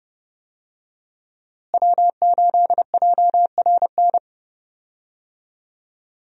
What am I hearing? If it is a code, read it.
Morse code, heard as a keyed tone, about 30 wpm, 700 Hz. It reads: W8JRN
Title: USA Call Signs - 5 Characters - Repeated 3x - 30wpm